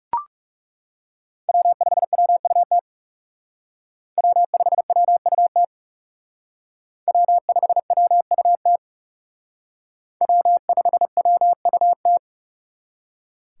Morse code, heard as a keyed tone, about 39 wpm, 700 Hz. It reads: W5WUT W5WUT W5WUT W5WUT